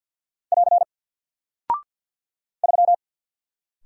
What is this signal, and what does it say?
Morse code, heard as a keyed tone, about 48 wpm, 700 Hz. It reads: <AR> 3
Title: Single Character (Letter:Number:Punctuation:Prosign) Alt 50wpm